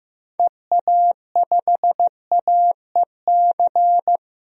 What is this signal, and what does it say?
Morse code, heard as a keyed tone, about 16 wpm, 690 Hz. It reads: EA5AEC
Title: DXCC-100 Call Signs - 6 Characters - 15wpm